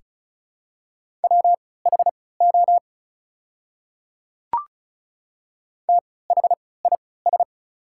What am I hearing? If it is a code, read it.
Morse code, heard as a keyed tone, about 36 wpm, spaced as for 20 wpm, 700 Hz. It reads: WHO THIS
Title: 3x Character Spacing - Top 200 Words - 35wpm